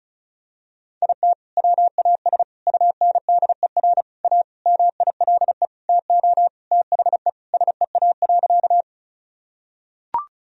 Morse code, heard as a keyed tone, about 34 wpm, 700 Hz. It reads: IT WAS UNDER A MILE TO THE SEA.
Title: Sentences from Top 1000 Words - Review Part 1 - 35wpm